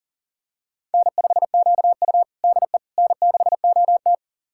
Morse code, heard as a keyed tone, about 40 wpm, 690 Hz. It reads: N5QU DE N6OT
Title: USA Callsign DE Callsign - 4 Characters - 40wpm